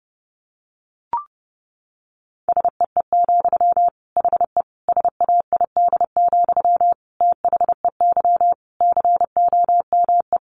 Morse code, heard as a keyed tone, about 30 wpm, 700 Hz. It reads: SEE, HE SAID, THEY COME
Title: Sentences from Top 700 Words 30wpm